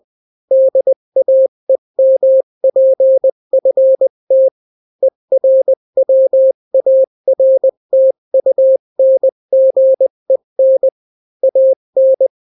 Morse code, heard as a keyed tone, about 20 wpm, 530 Hz. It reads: DAEMPFT ERWARTUNGEN AN
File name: daily / 18wpm